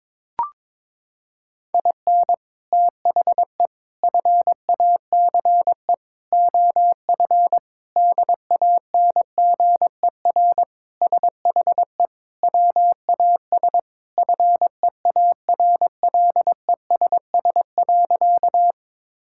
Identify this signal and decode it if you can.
Morse code, heard as a keyed tone, about 22 wpm, 700 Hz. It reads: IN THE FACE OF DANGER SHE WAS FEARLESS.
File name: practice / 22wpm / Sentences with 4-Word Phrases - 22wpm